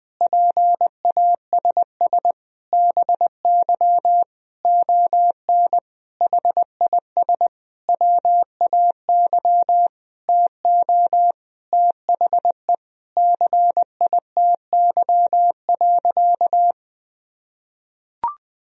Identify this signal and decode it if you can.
Morse code, heard as a keyed tone, about 20 wpm, 700 Hz. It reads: PASS BY ON HIS WAY TO THE CITY.
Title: Sentences from Top 1000 Words - Review Part 1 - 20wpm